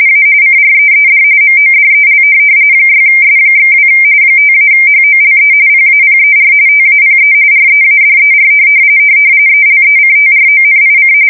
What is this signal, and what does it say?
RTTY, about 45.45 baud. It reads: A CREED MODEL 6S/2 5-HOLE PAPER TAPE READER ELECTRONIC
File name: fsk170 / teletype-RTTY